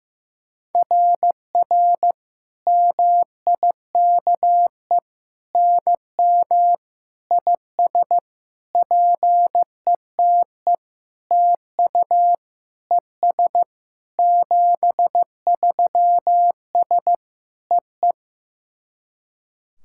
Morse code, heard as a keyed tone, about 15 wpm, 700 Hz. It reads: RR MIKE NM IS PETE TU ES 73S EE
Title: SOTA QSO Practice 15wpm